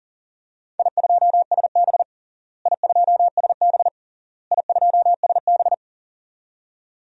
Morse code, heard as a keyed tone, about 40 wpm, 700 Hz. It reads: I2SB I2SB I2SB
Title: DXCC-100 Call Signs - 4 Characters - Repeated 3x - 40wpm